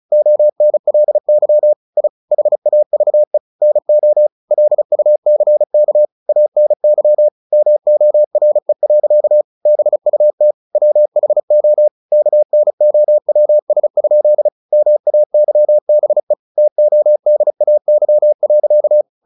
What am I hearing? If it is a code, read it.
Morse code, heard as a keyed tone, about 34 wpm, 600 Hz. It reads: ONLY I HAVE NO LUCK ANY MORE. BUT WHO KNOWS? MAYBE TODAY.
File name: the old man and the sea / the old man and the sea 1 35wpm